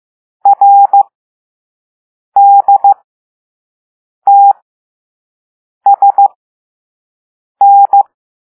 Morse code, heard as a keyed tone, about 15 wpm, spaced as for 5 wpm, 800 Hz. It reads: RDTSN